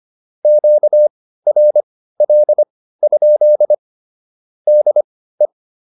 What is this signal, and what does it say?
Morse code, heard as a keyed tone, about 25 wpm, spaced as for 15 wpm, 600 Hz. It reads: QRL? DE